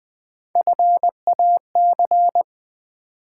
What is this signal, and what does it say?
Morse code, heard as a keyed tone, about 20 wpm, 700 Hz. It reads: FAC
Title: Build A Word - 1 Letter At A Time - 3 to 5-Letter Words - 20wpm